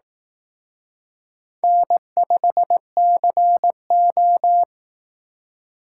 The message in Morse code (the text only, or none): N5CO